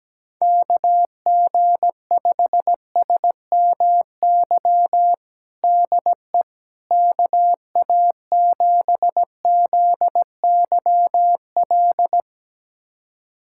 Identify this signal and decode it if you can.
Morse code, heard as a keyed tone, about 17 wpm, 700 Hz. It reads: KG5SMY DE KA7ZYL
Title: USA Callsign DE Callsign - 6 Characters - 17wpm